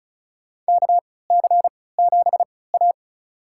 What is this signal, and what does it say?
Morse code, heard as a keyed tone, about 34 wpm, spaced as for 20 wpm, 700 Hz. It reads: KC7A
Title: USA Call Signs - 4 Characters - 3x Character Spacing - 35wpm